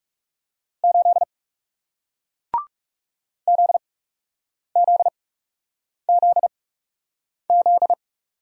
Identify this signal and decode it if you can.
Morse code, heard as a keyed tone, about 40 wpm, spaced as for 8 wpm, 700 Hz. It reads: 8 ZZZZ